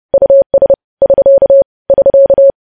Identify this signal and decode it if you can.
Morse code, heard as a keyed tone, about 30 wpm, 550 Hz. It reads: US <SK> <SK>
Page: news headlines in morse code at 30 wpm